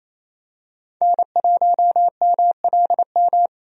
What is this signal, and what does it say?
Morse code, heard as a keyed tone, about 28 wpm, 700 Hz. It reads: N1MLM
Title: USA Call Signs - 5 Characters - 28wpm